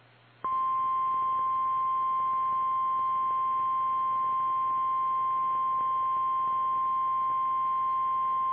Emergency Alert System (SAME) tones (the 1050 Hz attention tone). It is heard from 0.9 to 7.5 s.
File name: eas / KKPRKXPN / KKPRKXPN